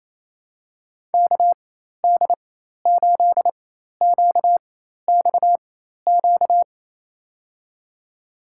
Morse code, heard as a keyed tone, about 28 wpm, spaced as for 13 wpm, 700 Hz. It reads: KD8QXQ
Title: USA Call Signs - 6 Characters - 4x Character Spacing - 28wpm